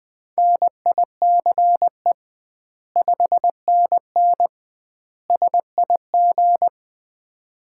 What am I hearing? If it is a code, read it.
Morse code, heard as a keyed tone, about 20 wpm, 700 Hz. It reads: NICE 5NN SIG
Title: Common QSO Phrases - 2x Word-Spacing - 20wpm